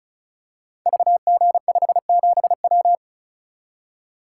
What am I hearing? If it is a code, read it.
Morse code, heard as a keyed tone, about 36 wpm, 700 Hz. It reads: VG57W
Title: Code Groups - 5 Letter:Numbers - 35wpm